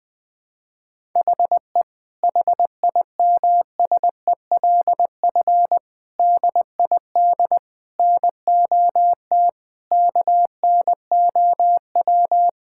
Morse code, heard as a keyed tone, about 21 wpm, 700 Hz. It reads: HE HIMSELF DID NOT KNOW
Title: Sentences from Top 1600 Words 20wpm